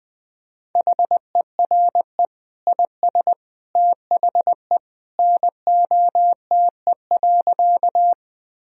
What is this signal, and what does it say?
Morse code, heard as a keyed tone, about 21 wpm, 700 Hz. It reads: HERE IS THE NOTE.
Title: Sentences from Top 1000 Words - Review Part 2 - 20wpm